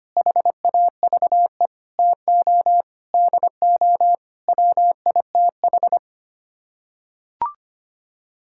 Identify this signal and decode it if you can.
Morse code, heard as a keyed tone, about 25 wpm, 700 Hz. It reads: HAVE TO DO WITH
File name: Sets of 9 Words - 25wpm